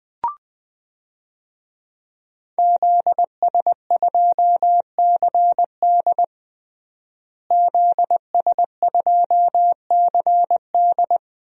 Morse code, heard as a keyed tone, about 20 wpm, 700 Hz. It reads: ZS2CD ZS2CD
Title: DXCC-100 Call Signs - 5 Characters - Repeated 2x - 20wpm